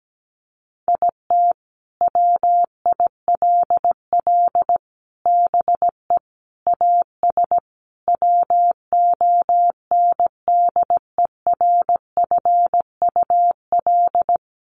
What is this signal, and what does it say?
Morse code, heard as a keyed tone, about 17 wpm, 700 Hz. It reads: IT WILL BE AS WONDERFUL
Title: Sentences from Top 1400 Words 17wpm